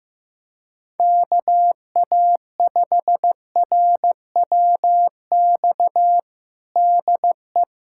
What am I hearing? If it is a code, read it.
Morse code, heard as a keyed tone, about 15 wpm, 700 Hz. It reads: KA5RWX DE
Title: USA Callsign DE Callsign - 6 Characters - 15wpm